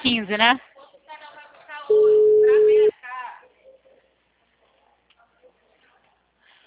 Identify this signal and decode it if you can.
Ringback tone (Europe, 425 Hz), with speech.